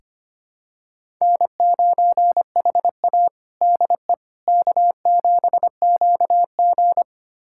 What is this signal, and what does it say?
Morse code, heard as a keyed tone, about 25 wpm, 700 Hz. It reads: N9HA DE K7QG
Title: USA Callsign DE Callsign - 4 Characters - 25wpm